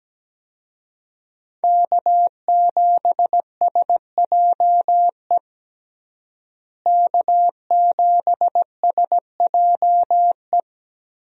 Morse code, heard as a keyed tone, about 17 wpm, 700 Hz. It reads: K7SJE K7SJE
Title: USA Call Signs - 5 Characters - Repeated 2x - 17wpm